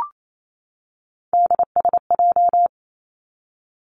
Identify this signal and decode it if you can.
Morse code, heard as a keyed tone, about 28 wpm, 700 Hz. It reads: DSJ
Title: Three Challenging Letter-Number Combinations 28wpm